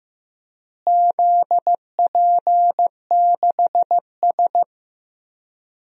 Morse code, heard as a keyed tone, about 15 wpm, 700 Hz. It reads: ZP6S
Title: DXCC-100 Call Signs - 4 Characters - Repeated 3x - 15wpm